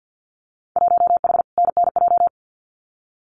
Morse code, heard as a keyed tone, about 50 wpm, 700 Hz. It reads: 1H NNJ
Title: Winter Field Day Exchanges 50wpm